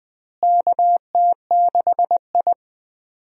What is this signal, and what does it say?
Morse code, heard as a keyed tone, about 20 wpm, 700 Hz. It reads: KT6I
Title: USA Call Signs - 4 Characters - Repeated 3x - 20wpm